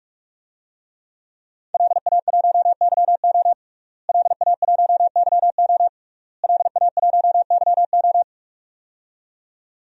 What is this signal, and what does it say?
Morse code, heard as a keyed tone, about 44 wpm, 700 Hz. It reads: LA1YO LA1YO LA1YO
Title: DXCC-100 Call Signs - 5 Characters - Repeated 3x - 45wpm